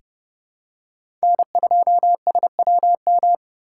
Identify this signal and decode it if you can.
Morse code, heard as a keyed tone, about 30 wpm, 700 Hz. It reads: N2SWM
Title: USA Call Signs - 3x Character Spacing - 30wpm